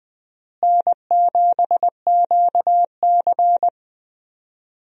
Morse code, heard as a keyed tone, about 20 wpm, 700 Hz. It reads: N7QC
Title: USA Call Signs - 4 Characters - Repeated 2x - 20wpm